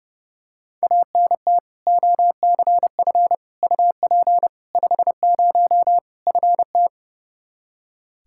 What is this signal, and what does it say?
Morse code, heard as a keyed tone, about 32 wpm, 700 Hz. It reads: ANT OCF UP 50 FT
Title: K1USN SST QSO Practice 30wpm